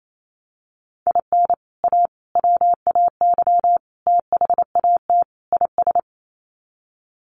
Morse code, heard as a keyed tone, about 28 wpm, 700 Hz. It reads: IN A WAY THAT IS
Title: Sets of 5 Words Encore 28wpm